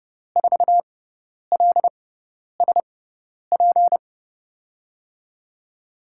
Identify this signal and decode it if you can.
Morse code, heard as a keyed tone, about 30 wpm, spaced as for 10 wpm, 700 Hz. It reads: 4LSP
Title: USA Call Signs - 6x Character Spacing - 30wpm